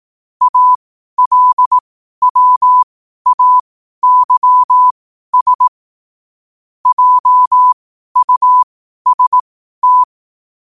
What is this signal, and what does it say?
Morse code, heard as a keyed tone, about 18 wpm, spaced as for 12 wpm, 1000 Hz. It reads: ALWAYS JUST
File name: qotdfiles / 12wpm